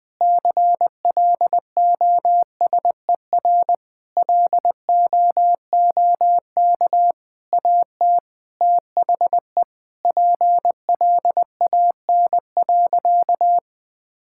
Morse code, heard as a keyed tone, about 21 wpm, 700 Hz. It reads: CLOSER LOOK AT THE PLAN.
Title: Sentences with 6-Word Phrases - 20wpm